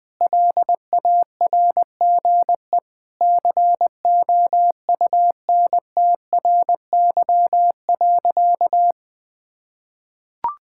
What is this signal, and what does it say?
Morse code, heard as a keyed tone, about 20 wpm, 700 Hz. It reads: LARGE COUNTRY.
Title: Sentences with 4-Word Phrases - Spoken Beginning - 20wpm